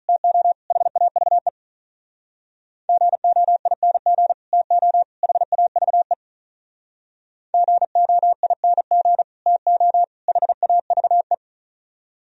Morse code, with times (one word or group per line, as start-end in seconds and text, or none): 0.1-0.5 TO
0.7-1.5 HAVE
2.9-4.3 GOING
4.5-5.0 TO
5.2-6.1 HAVE
7.5-9.2 GOING
9.5-10.0 TO
10.3-11.3 HAVE